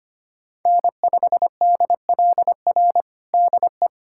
Morse code, heard as a keyed tone, about 25 wpm, 700 Hz. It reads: N5DLR DE
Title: USA Callsign DE Callsign - 5 Characters - 25wpm